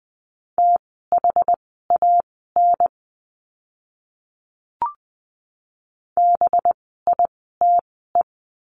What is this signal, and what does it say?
Morse code, heard as a keyed tone, about 20 wpm, spaced as for 15 wpm, 700 Hz. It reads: THAN BITE
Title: 185 Lesson 30f - Letter b - Words 20wpm